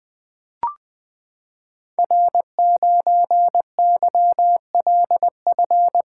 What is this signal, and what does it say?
Morse code, heard as a keyed tone, about 20 wpm, 700 Hz. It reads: R9YLF
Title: Code Groups - 5 Letter:Numbers - 20wpm